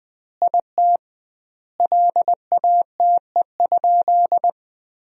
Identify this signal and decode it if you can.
Morse code, heard as a keyed tone, about 21 wpm, 700 Hz. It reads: IT LATE?
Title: 2x Word-Spacing - Sentences from Top 300 Words - 20wpm